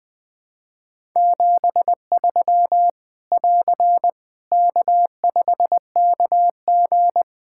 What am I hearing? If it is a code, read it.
Morse code, heard as a keyed tone, about 21 wpm, 700 Hz. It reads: 73 <AR> K5KG
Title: K1USN SST QSO Practice 20wpm